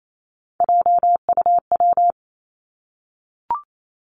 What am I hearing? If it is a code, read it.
Morse code, heard as a keyed tone, about 28 wpm, 700 Hz. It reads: JUW
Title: Three Challenging Letter-Number Combinations 28wpm